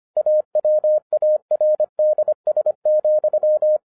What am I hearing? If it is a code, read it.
Morse code, heard as a keyed tone, about 25 wpm, 610 Hz. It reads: AWARDS,